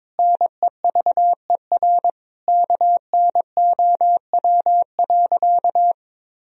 Morse code, heard as a keyed tone, about 22 wpm, 700 Hz. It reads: NEVER KNOW.
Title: Sentences from Top 300 Words and Abbreviations 22wpm